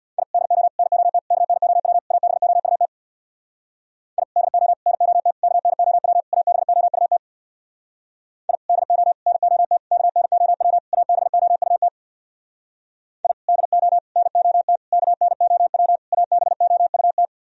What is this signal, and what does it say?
Morse code, heard as a keyed tone, about 62 wpm, 700 Hz. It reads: I DO NOT KNOW ABOUT I DO NOT KNOW ABOUT I DO NOT KNOW ABOUT I DO NOT KNOW ABOUT
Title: Speed-Racing - Sets of 5 Words 50wpm